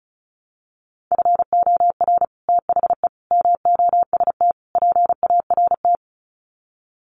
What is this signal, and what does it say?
Morse code, heard as a keyed tone, about 35 wpm, 700 Hz. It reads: FOR THE MOST PART
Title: Sets of 4 Words 35wpm